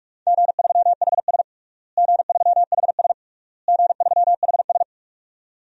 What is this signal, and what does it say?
Morse code, heard as a keyed tone, about 46 wpm, 700 Hz. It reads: G3HS G3HS G3HS
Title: DXCC-100 Call Signs - 4 Characters - Repeated 3x - 45wpm